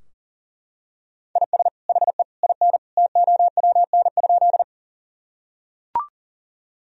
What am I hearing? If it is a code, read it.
Morse code, heard as a keyed tone, about 40 wpm, 700 Hz. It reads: IS HE IN TOWN?
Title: Sentences from Top 1000 Words - Review Part 2 - 40wpm